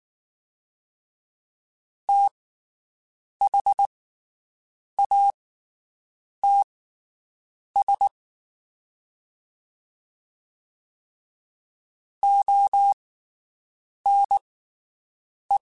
Morse code, heard as a keyed tone, about 19 wpm, spaced as for 7 wpm, 800 Hz. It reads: THATS ONE
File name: QSOs Spaced